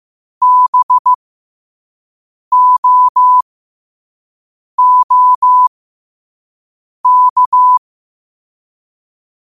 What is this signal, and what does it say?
Morse code, heard as a keyed tone, about 15 wpm, spaced as for 5 wpm, 1000 Hz. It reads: BOOK
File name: qotdfiles / 5wpm